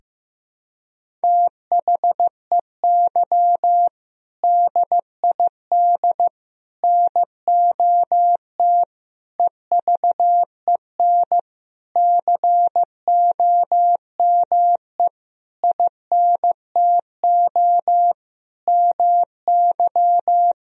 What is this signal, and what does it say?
Morse code, heard as a keyed tone, about 15 wpm, 700 Hz. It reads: THEY DID NOT EVEN COME INTO MY
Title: Sentences from Top 1000 Words - Review Part 1 - 15wpm